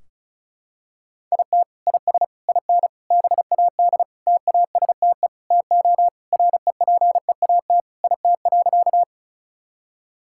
Morse code, heard as a keyed tone, about 36 wpm, 700 Hz. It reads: IT IS IN BAD TASTE TO REPEAT IT.
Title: Sentences from Top 1600 Words 35wpm